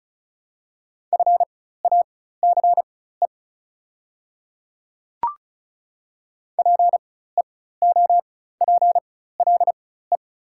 Morse code, heard as a keyed tone, about 33 wpm, spaced as for 16 wpm, 690 Hz. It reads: FACE PEOPLE
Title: 4x Character Spacing - Top 300 Words - 35wpm